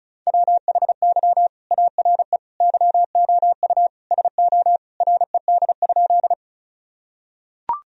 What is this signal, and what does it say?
Morse code, heard as a keyed tone, about 34 wpm, 700 Hz. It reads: WHY ARE YOU SO RED?
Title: Sentences from Top 1000 Words - Review Part 2 - 35wpm